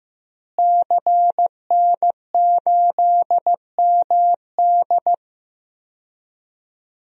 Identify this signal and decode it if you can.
Morse code, heard as a keyed tone, about 15 wpm, 700 Hz. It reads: CN8MD
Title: DXCC-100 Call Signs - 5 Characters - Repeated 2x - 15wpm